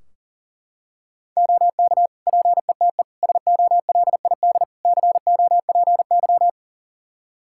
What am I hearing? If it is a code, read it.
Morse code, heard as a keyed tone, about 39 wpm, 700 Hz. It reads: OK PETE SOLID COPY
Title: Practice QSO - Exchange 3 - More Info - 40wpm